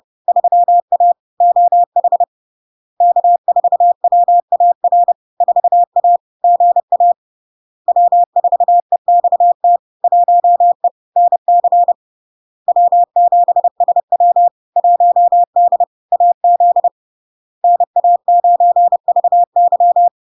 Morse code, heard as a keyed tone, about 30 wpm, 700 Hz. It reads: 3A OH K4WAR 4A GA W4EXT 1E NC W7SW 1D AZ NA9VY
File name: LARK-FD-2022-Log-in-CW-30wpm-700-Hz